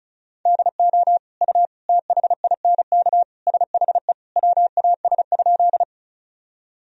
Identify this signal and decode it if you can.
Morse code, heard as a keyed tone, about 34 wpm, 700 Hz. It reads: DO U THINK SHE WAS?
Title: Sentences from Top 200 Words and Abbreviations 35wpm